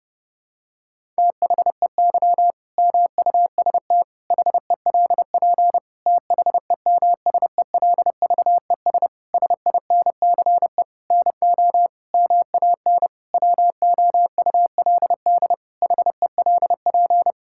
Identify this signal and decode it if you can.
Morse code, heard as a keyed tone, about 28 wpm, 700 Hz. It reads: THEY MUST HELP THEMSELVES SINCE NO MAN WOULD HELP
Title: Sentences from Top 1000 Words - Review Part 5 - 30wpm